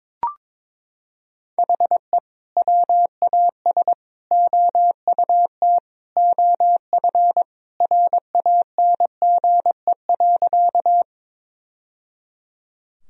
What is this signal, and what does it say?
Morse code, heard as a keyed tone, about 22 wpm, 700 Hz. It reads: HE WAS OUT OF RANGE.
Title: Sentences from Top 1000 Words - Review Part 4 - 22wpm